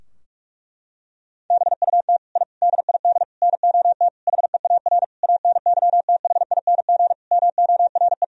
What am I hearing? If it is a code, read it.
Morse code, heard as a keyed tone, about 41 wpm, 700 Hz. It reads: BUT I DID NOT HEAR ANYTHING MORE
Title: Sentences from Top 1000 Words - Review Part 4 - 45wpm